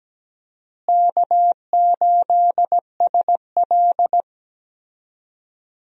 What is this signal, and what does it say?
Morse code, heard as a keyed tone, about 17 wpm, 700 Hz. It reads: K8SL
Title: USA Call Signs - 4 Characters - Repeated 2x - 17wpm